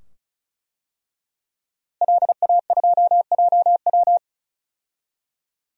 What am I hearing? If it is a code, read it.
Morse code, heard as a keyed tone, about 36 wpm, 700 Hz. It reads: LA2JW
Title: DXCC-100 Call Signs - 5 Characters - Repeated 2x - 35wpm